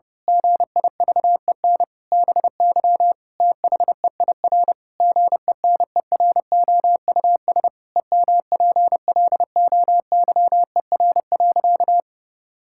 Morse code, heard as a keyed tone, about 32 wpm, 700 Hz. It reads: GIVEN BY THEIR GENEROUS EMPLOYER.
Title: Pangram Challenge v2 - 30wpm